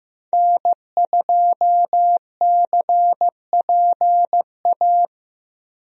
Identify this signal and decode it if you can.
Morse code, heard as a keyed tone, about 15 wpm, 700 Hz. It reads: N2CPA
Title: USA Call Signs - 5 Characters - Repeated 3x - 15wpm